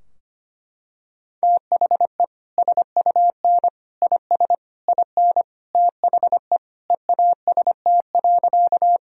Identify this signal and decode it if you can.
Morse code, heard as a keyed tone, about 25 wpm, 700 Hz. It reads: THE SUN IS IN THE EAST.
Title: Sentences from Top 1000 Words - Review Part 3 - 25wpm